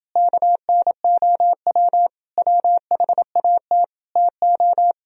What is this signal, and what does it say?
Morse code, heard as a keyed tone, about 30 wpm, 700 Hz. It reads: KNOW WHAT TO